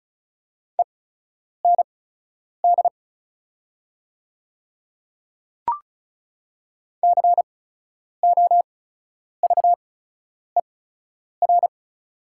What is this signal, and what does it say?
Morse code, heard as a keyed tone, about 36 wpm, spaced as for 10 wpm, 700 Hz. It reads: END COVER
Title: 8x Character Spacing - Top 500 Words - 35wpm